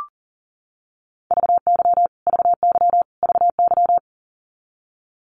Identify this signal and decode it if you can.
Morse code, heard as a keyed tone, about 40 wpm, 700 Hz. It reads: VY VY VY
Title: Abbreviations Repeated 3x - 40wpm